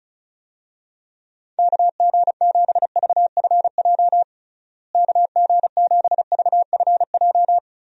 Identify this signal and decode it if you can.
Morse code, heard as a keyed tone, about 36 wpm, 700 Hz. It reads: KG7VFJ KG7VFJ
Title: USA Call Signs - 6 Characters - Repeated 3x - 35wpm